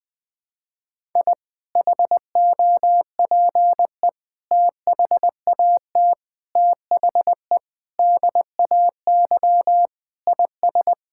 Morse code, heard as a keyed tone, about 21 wpm, 700 Hz. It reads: I HOPE THAT THE DAY IS
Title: Sentences with 2-Word Phrases - 20wpm